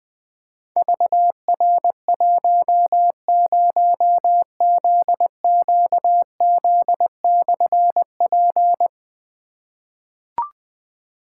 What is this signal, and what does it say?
Morse code, heard as a keyed tone, about 20 wpm, 700 Hz. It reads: VR10ZQZ/P
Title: Difficult Call Signs 20wpm